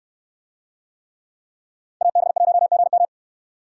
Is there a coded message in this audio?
Yes: Morse code from 2.0 to 3.1 s, heard as a keyed tone, about 69 wpm, 700 Hz. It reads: AB1GM